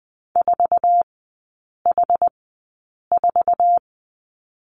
Morse code, heard as a keyed tone, about 20 wpm, spaced as for 8 wpm, 700 Hz. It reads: 4H4
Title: 060 Lesson 11e - Letter h - Three Characters 20wpm